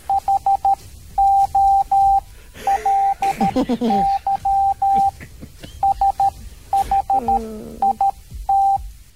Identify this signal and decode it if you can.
Morse code, heard as a keyed tone, about 13 wpm, spaced as for 11 wpm, 790 Hz, with speech. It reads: HOLY SHIT